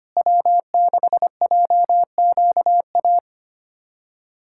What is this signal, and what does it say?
Morse code, heard as a keyed tone, about 25 wpm, 700 Hz. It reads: W6JQA
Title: SOTA QSO Practice 25wpm